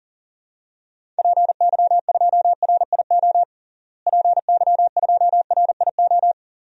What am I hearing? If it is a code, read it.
Morse code, heard as a keyed tone, about 40 wpm, 700 Hz. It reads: PY2RIO PY2RIO
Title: DXCC-100 Call Signs - 6 Characters - Repeated 3x - 40wpm